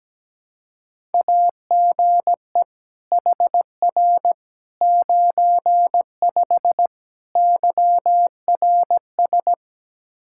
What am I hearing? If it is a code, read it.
Morse code, heard as a keyed tone, about 17 wpm, 700 Hz. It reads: AGE HR 95 YRS
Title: Practice QSO - Exchange 3 - More Info - 17wpm